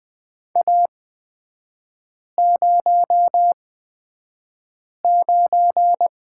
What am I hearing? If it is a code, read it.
Morse code, heard as a keyed tone, about 20 wpm, spaced as for 5 wpm, 700 Hz. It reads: A09
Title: 224 Lesson 36e - BT - Three Characters 20wpm